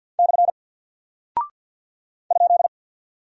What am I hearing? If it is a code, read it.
Morse code, heard as a keyed tone, about 48 wpm, 700 Hz. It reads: / ?
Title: Single Character (Letter:Number:Punctuation:Prosign) Alt 50wpm